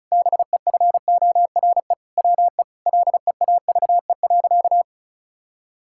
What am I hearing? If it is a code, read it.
Morse code, heard as a keyed tone, about 34 wpm, 700 Hz. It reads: BEFORE WE LEAVE.
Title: Sentences from Top 1000 Words - Review Part 5 - 35wpm